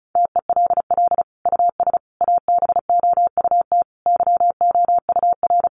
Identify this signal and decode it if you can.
Morse code, heard as a keyed tone, about 34 wpm, 700 Hz. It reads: TELL US ABOUT YOUR
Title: Sentences from Top 1200 Words 35wpm